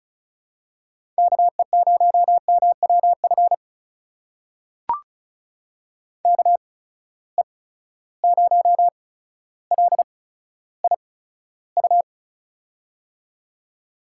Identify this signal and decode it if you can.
Morse code, heard as a keyed tone, about 36 wpm, spaced as for 15 wpm, 700 Hz. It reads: KE0MWF KE0LIU
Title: USA Call Signs - 6 Characters - 8x Character Spacing - 35wpm